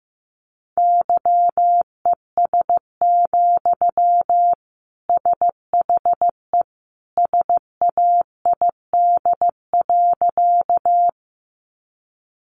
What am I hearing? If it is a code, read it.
Morse code, heard as a keyed tone, about 15 wpm, 700 Hz. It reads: YES, SHE SAID.
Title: Sentences from Top 500 Words 15wpm